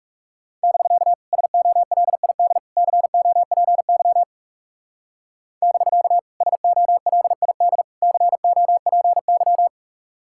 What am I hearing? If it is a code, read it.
Morse code, heard as a keyed tone, about 38 wpm, 700 Hz. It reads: <BK> SOLID COPY <BK> SOLID COPY